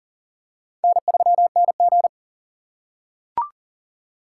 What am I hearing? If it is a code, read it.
Morse code, heard as a keyed tone, about 40 wpm, 700 Hz. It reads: N3NG
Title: USA Call Signs - 4 Characters - 3x Character Spacing - 40wpm